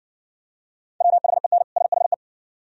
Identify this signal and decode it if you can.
Morse code, heard as a keyed tone, about 60 wpm, 700 Hz. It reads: WHEN SHE